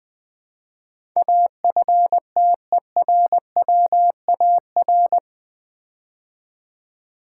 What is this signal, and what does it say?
Morse code, heard as a keyed tone, about 20 wpm, 700 Hz. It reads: AFTERWAR
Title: Build A Word - 4 Letters At A Time - 10-Letter Words - 20wpm